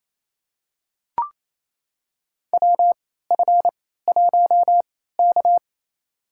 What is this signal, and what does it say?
Morse code, heard as a keyed tone, about 28 wpm, spaced as for 16 wpm, 700 Hz. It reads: WF1K
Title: USA Call Signs - 4 Characters - 3x Character Spacing - 28wpm